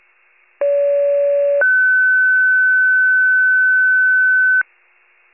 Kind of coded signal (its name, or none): two-tone page